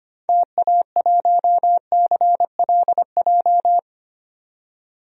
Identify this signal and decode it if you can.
Morse code, heard as a keyed tone, about 25 wpm, 700 Hz. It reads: TA1CLJ
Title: DXCC-100 Call Signs - 6 Characters - 25wpm